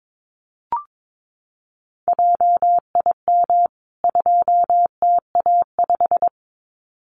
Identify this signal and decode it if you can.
Morse code, heard as a keyed tone, about 22 wpm, 700 Hz. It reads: JIM 2TA5
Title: CWT Exchange 22wpm